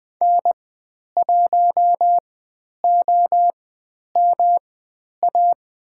Morse code, heard as a keyed tone, about 20 wpm, spaced as for 10 wpm, 700 Hz. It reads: N1OMA
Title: 109 Lesson 18g - Letter m - Call Signs 20wpm